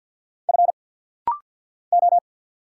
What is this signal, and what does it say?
Morse code, heard as a keyed tone, about 50 wpm, 700 Hz. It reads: F O
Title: Single Letter-Number - Rapid-Fire 50wpm